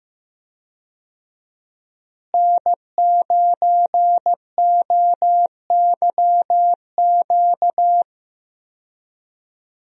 Morse code, heard as a keyed tone, about 15 wpm, 700 Hz. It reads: N9OYQ
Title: USA Call Signs - 5 Characters - Repeated 3x - 15wpm